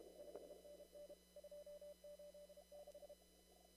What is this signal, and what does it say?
Morse code, heard as a keyed tone, about 33 wpm, 590 Hz. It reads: 5NN 286
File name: CW / kt5j wpx cw-op-k5pi